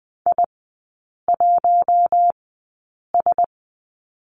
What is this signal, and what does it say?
Morse code, heard as a keyed tone, about 20 wpm, spaced as for 8 wpm, 700 Hz. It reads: I1S
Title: 041 Lesson 8e - Number 1 - Three Characters 20wpm